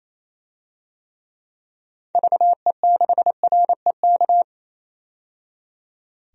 Morse code, heard as a keyed tone, about 28 wpm, 700 Hz. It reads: VE6REK
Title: DXCC-100 Call Signs - 6 Characters - 28wpm